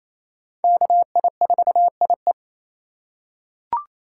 Morse code, heard as a keyed tone, about 28 wpm, 700 Hz. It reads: KI4IE
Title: USA Call Signs - 5 Characters - 4x Character Spacing - 28wpm